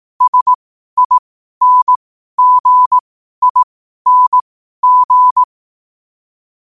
Morse code, heard as a keyed tone, about 18 wpm, spaced as for 12 wpm, 1000 Hz. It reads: SINGING
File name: qotdfiles / 12wpm